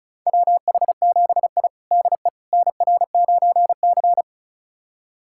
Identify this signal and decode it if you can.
Morse code, heard as a keyed tone, about 34 wpm, 700 Hz. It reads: WH7I DE NR9C
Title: USA Callsign DE Callsign - 4 Characters - 35wpm